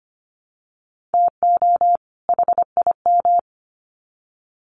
Morse code, heard as a keyed tone, about 25 wpm, 700 Hz. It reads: TO HIM
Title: Sets of 2 Words 25wpm